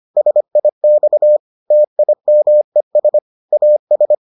Morse code, heard as a keyed tone, about 25 wpm, 600 Hz. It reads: SIX TIMES AS